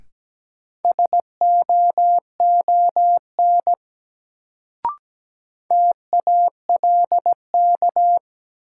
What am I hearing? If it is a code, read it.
Morse code, heard as a keyed tone, about 17 wpm, 700 Hz. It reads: SOON TALK
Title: Sentences from Top 400 Words - New Word Drill - 17wpm